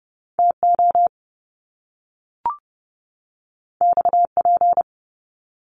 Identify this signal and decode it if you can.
Morse code, heard as a keyed tone, about 30 wpm, 700 Hz. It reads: TO XP
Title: Top Two Letter Combinations in Words 30wpm